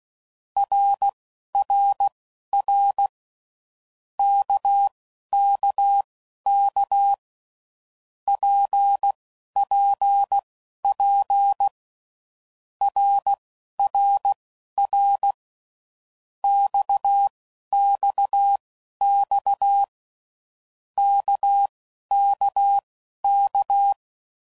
Morse code, heard as a keyed tone, about 16 wpm, spaced as for 11 wpm, 800 Hz. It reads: RRR KKK PPP RRR XXX KKK
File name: Groups / Basic Letter Groups / RKPX